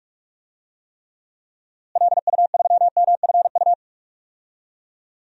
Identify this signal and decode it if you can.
Morse code, heard as a keyed tone, about 46 wpm, 700 Hz. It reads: LU3MFU
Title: DXCC-100 Call Signs - 6 Characters - Repeated 2x - 45wpm